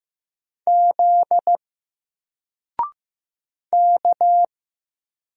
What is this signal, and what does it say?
Morse code, heard as a keyed tone, about 15 wpm, 700 Hz. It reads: Z K
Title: Single Character (Letter:Number:Punctuation:Prosign) Alt 15wpm